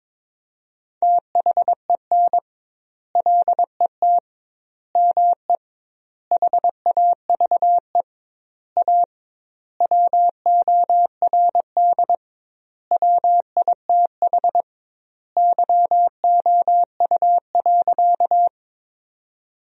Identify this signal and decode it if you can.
Morse code, heard as a keyed tone, about 22 wpm, 700 Hz. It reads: THEN LET ME HAVE A WORD WITH YOU.
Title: 2x Word-Spacing - Sentences from Top 300 Words - 22wpm